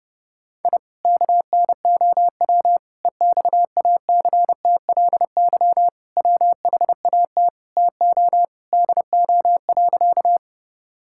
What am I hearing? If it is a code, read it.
Morse code, heard as a keyed tone, about 30 wpm, 700 Hz. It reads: I KNOW EXACTLY WHAT TO DO.
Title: Sentences from Top 1000 Words - Review Part 6 - 30wpm